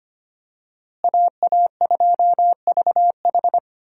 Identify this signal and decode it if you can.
Morse code, heard as a keyed tone, about 25 wpm, 700 Hz. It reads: AA2VH
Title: USA Call Signs - 5 Characters - Repeated 3x - 25wpm